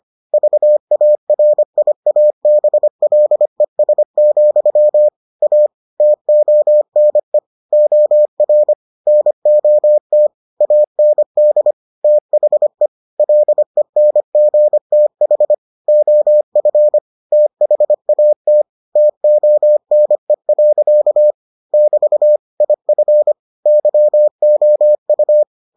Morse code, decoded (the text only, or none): VARIABLES, A TONE OR NOT AND THE LENGTH OF THAT TONE. <BT> IF YOU